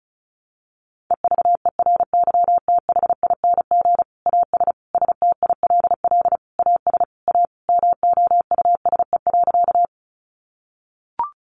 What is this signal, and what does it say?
Morse code, heard as a keyed tone, about 34 wpm, 700 Hz. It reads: EVERYTHING AS STILL AS A MOUSE.
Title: Sentences from Top 700 Words 35wpm